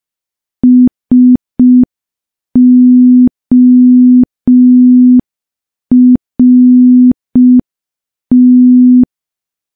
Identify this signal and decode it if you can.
Morse code, heard as a keyed tone, about 5 wpm, 250 Hz. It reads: SORT
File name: morse / morse-training-05WPM